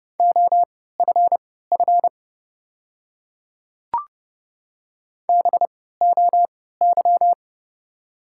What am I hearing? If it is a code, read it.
Morse code, heard as a keyed tone, about 30 wpm, spaced as for 17 wpm, 690 Hz. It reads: OFF BOY
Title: 3x Character Spacing - Top 200 Words - 30wpm